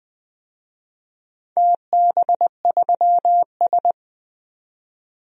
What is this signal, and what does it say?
Morse code, heard as a keyed tone, about 20 wpm, 700 Hz. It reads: TB3S
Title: DXCC-100 Call Signs - 4 Characters - 20wpm